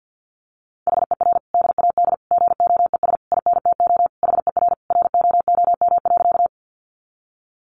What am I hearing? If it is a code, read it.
Morse code, heard as a keyed tone, about 50 wpm, 700 Hz. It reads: HER DAD GOES INTO HER ROOM.
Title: Additional Easy Phrases 50wpm